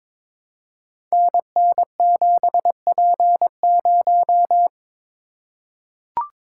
Morse code, heard as a keyed tone, about 22 wpm, 700 Hz. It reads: NN7P0
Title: Code Groups - 5 Letter:Numbers - 22wpm